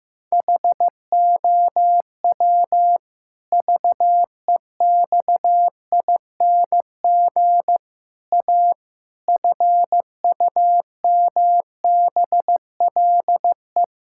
Morse code, heard as a keyed tone, about 16 wpm, 700 Hz. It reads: HOW VEXING A FUMBLE
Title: Pangram Challenge v2 - 15wpm